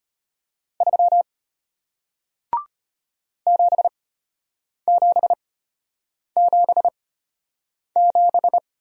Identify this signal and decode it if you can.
Morse code, heard as a keyed tone, about 34 wpm, spaced as for 8 wpm, 700 Hz. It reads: 3 7777